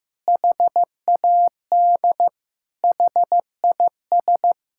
Morse code, heard as a keyed tone, about 15 wpm, 700 Hz. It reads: HAD HIS
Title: Sentences from Top 1600 Words 15wpm